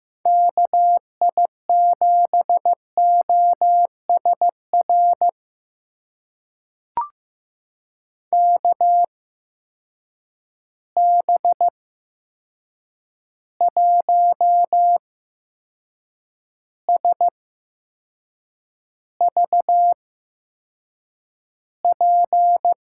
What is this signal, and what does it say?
Morse code, heard as a keyed tone, about 15 wpm, spaced as for 6 wpm, 700 Hz. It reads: KI7OSR KB1SVP
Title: USA Call Signs - 6 Characters - 8x Character Spacing - 15wpm